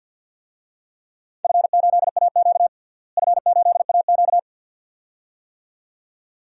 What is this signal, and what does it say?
Morse code, heard as a keyed tone, about 52 wpm, 700 Hz. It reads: F8AQ F8AQ
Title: DXCC-100 Call Signs - 4 Characters - Repeated 2x - 50wpm